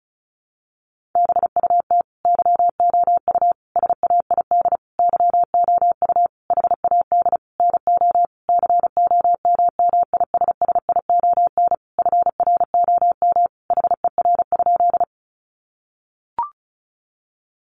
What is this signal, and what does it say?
Morse code, heard as a keyed tone, about 34 wpm, 700 Hz. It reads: BUT YOU SAID YOU HAD NO COMMISSION FROM HER?
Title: Sentences from Top 1400 Words 35wpm